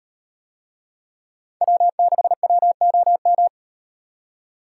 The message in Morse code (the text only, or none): W6WOM